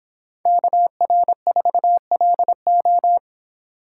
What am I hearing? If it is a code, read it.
Morse code, heard as a keyed tone, about 26 wpm, 700 Hz. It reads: KR4LO